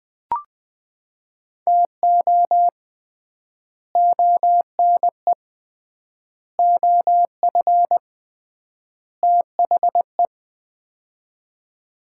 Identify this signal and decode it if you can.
Morse code, heard as a keyed tone, about 21 wpm, 700 Hz. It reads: TO ONE OF THE
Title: 3x Word-Spacing - Sets of 4 Words - 20wpm